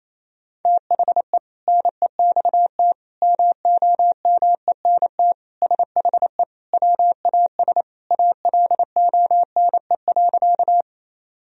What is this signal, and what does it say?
Morse code, heard as a keyed tone, about 28 wpm, 700 Hz. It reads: THE NEXT MOMENT SHE WAS ALONE.
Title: Sentences from Top 1000 Words - Review Part 1 - 28wpm